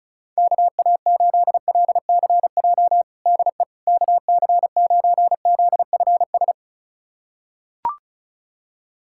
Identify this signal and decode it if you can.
Morse code, heard as a keyed tone, about 34 wpm, 700 Hz. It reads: KA8LCJ DE KC9ZFS
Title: USA Callsign DE Callsign - 6 Characters - 35wpm